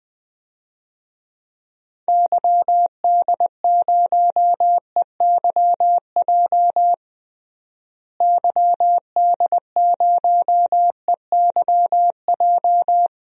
Morse code, heard as a keyed tone, about 21 wpm, 700 Hz. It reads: YD0EYJ YD0EYJ
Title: DXCC-100 Call Signs - 6 Characters - Repeated 2x - 20wpm